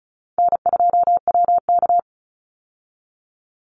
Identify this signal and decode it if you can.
Morse code, heard as a keyed tone, about 36 wpm, 700 Hz. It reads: N2WK
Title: Call Signs - 35wpm